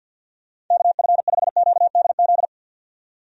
Morse code, heard as a keyed tone, about 46 wpm, 700 Hz. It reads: KF5QDZ
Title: USA Call Signs - 6 Characters - 50wpm